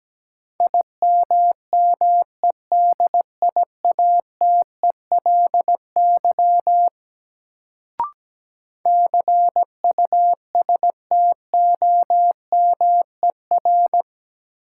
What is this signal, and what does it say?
Morse code, heard as a keyed tone, about 17 wpm, 700 Hz. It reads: IMMEDIATELY CUSTOMER
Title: Sentences from Top 1100 Words - New Word Drill - 17wpm